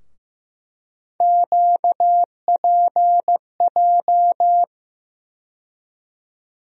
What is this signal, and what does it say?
Morse code, heard as a keyed tone, about 15 wpm, 700 Hz. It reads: QPJ
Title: USA Call Sign Postfixes - 3 Letters 15wpm